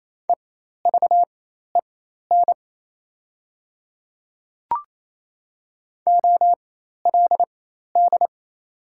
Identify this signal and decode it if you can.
Morse code, heard as a keyed tone, about 28 wpm, spaced as for 13 wpm, 700 Hz. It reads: EVEN OLD
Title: 4x Character Spacing - Top 300 Words - 28wpm